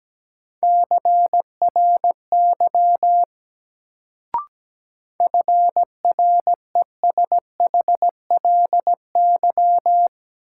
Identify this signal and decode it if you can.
Morse code, heard as a keyed tone, about 17 wpm, 700 Hz. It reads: CRY FRESHLY
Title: Sets of 5 Words Encore - New Word Drill - 17wpm